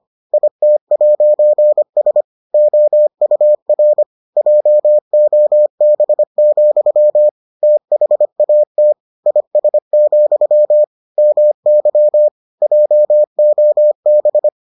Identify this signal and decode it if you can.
Morse code, heard as a keyed tone, about 25 wpm, 600 Hz. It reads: IT'S OUR JOB, THAT IS, MY JOB